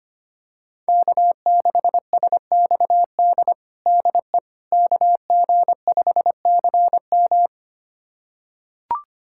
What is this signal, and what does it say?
Morse code, heard as a keyed tone, about 25 wpm, 700 Hz. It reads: K6SXD DE KG5CM
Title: USA Callsign DE Callsign - 5 Characters - 25wpm